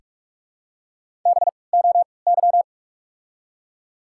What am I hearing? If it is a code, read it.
Morse code, heard as a keyed tone, about 46 wpm, spaced as for 25 wpm, 700 Hz. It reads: BOY
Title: 3x Character Spacing - Top 300 Words - 45wpm